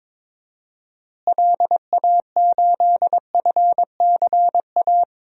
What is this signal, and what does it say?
Morse code, heard as a keyed tone, about 22 wpm, 700 Hz. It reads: LA8FCA
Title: DXCC-100 Call Signs - 6 Characters - Repeated 2x - 22wpm